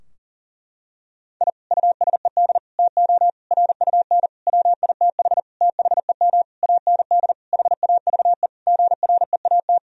Morse code, heard as a keyed tone, about 40 wpm, 700 Hz. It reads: I USED TO RUN WITH THEM AND HAVE GREAT
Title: Sentences from Top 1000 Words - Review Part 4 - 40wpm